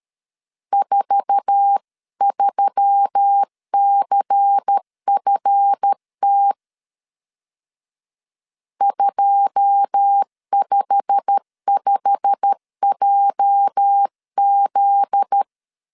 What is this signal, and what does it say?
Morse code, heard as a keyed tone, about 13 wpm, 800 Hz. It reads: 43CFT 255JZ
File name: CW / DET / 1 9 1500C S C 12wpm 3811